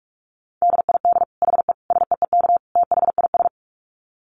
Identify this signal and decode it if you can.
Morse code, heard as a keyed tone, about 44 wpm, 700 Hz. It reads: DID HE SEEK THIS
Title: Dit Challenge 45wpm